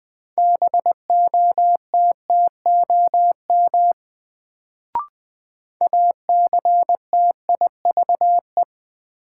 Morse code, heard as a keyed tone, about 20 wpm, 700 Hz. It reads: BOTTOM ACTIVE
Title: Sentences from Top 900 Words - New Word Drill - 20wpm